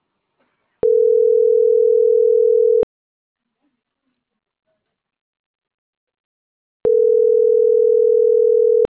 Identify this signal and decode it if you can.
Ringback tone (North America), 440 + 480 Hz.